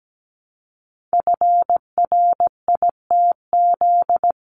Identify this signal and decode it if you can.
Morse code, heard as a keyed tone, about 17 wpm, 700 Hz. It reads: FRITZ
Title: CWT Exchange 17wpm